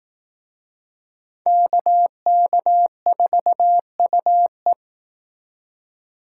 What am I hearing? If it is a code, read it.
Morse code, heard as a keyed tone, about 18 wpm, 700 Hz. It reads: KK4UE
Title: Speed Racing - USA 2x2 Call Signs 15wpm